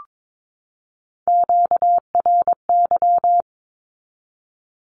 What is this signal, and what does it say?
Morse code, heard as a keyed tone, about 22 wpm, 700 Hz. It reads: QRY
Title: Q-Codes 22wpm